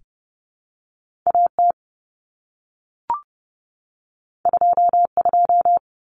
Morse code, heard as a keyed tone, about 30 wpm, 700 Hz. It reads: AT 22
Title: Common Cut Numbers - 2-Digit 30wpm